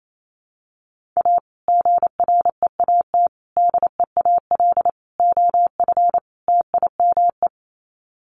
Morse code, heard as a keyed tone, about 28 wpm, 700 Hz. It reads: A GREAT DEAL OF TIME
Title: Sets of 5 Words Encore2 28wpm